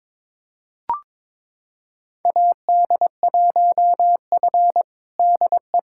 Morse code, heard as a keyed tone, about 21 wpm, 700 Hz. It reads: AD1F DE
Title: USA Callsign DE Callsign - 4 Characters - 22wpm